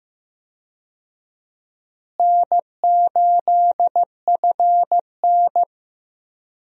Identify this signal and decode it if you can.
Morse code, heard as a keyed tone, about 15 wpm, 700 Hz. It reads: N8FN
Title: USA Call Signs - 4 Characters - Repeated 2x - 15wpm